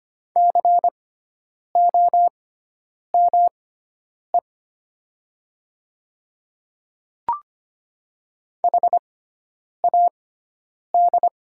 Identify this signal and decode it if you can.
Morse code, heard as a keyed tone, about 25 wpm, spaced as for 9 wpm, 700 Hz. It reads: COME HAD
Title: 6x Character Spacing - Top 500 Words - 25wpm